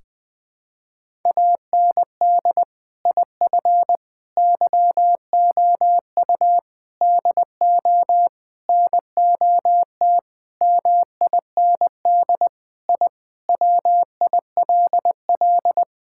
Morse code, heard as a keyed tone, about 21 wpm, 700 Hz. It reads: AND IF YOU DO NOT MIND I WILL
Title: Sentences with 5-Word Phrases - 20wpm